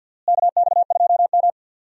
Morse code, heard as a keyed tone, about 52 wpm, 700 Hz. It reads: KX2M
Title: USA Call Signs - 4 Characters - Repeated 3x - 50wpm